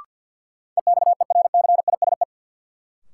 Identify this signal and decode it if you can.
Morse code, heard as a keyed tone, about 50 wpm, 700 Hz. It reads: EXERCISE
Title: Sentences from Top 1200 Words - New Word Drill Alt - 50wpm